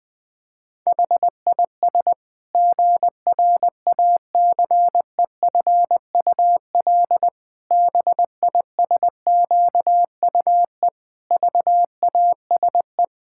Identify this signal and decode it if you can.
Morse code, heard as a keyed tone, about 21 wpm, 700 Hz. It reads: HIS GRACEFUL BISQUE VASE
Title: Pangram Challenge v2 - 20wpm